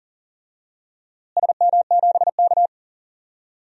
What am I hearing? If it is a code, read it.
Morse code, heard as a keyed tone, about 40 wpm, 700 Hz. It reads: SM7K
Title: DXCC-100 Call Signs - 4 Characters - Repeated 2x - 40wpm